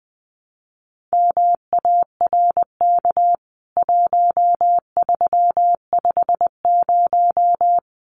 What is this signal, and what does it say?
Morse code, heard as a keyed tone, about 20 wpm, 700 Hz. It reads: MARK 1350
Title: CWT Exchange 20wpm